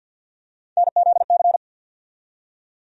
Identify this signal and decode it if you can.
Morse code, heard as a keyed tone, about 54 wpm, 700 Hz. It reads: NZC
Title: USA Call Sign Postfixes - 3 Letters 50wpm